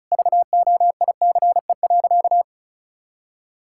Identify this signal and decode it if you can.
Morse code, heard as a keyed tone, about 36 wpm, 700 Hz. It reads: VOICE.
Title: Sentences from Top 1600 Words 35wpm